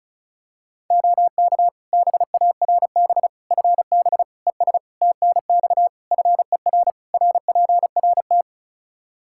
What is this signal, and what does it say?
Morse code, heard as a keyed tone, about 36 wpm, 700 Hz. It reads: OK BARB FB ES TNX FER RPRT
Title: Practice QSO - Exchange 2 - Info - 35wpm